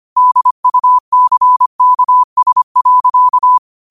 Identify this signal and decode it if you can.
Morse code, heard as a keyed tone, about 25 wpm, 1000 Hz. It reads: DUCKS.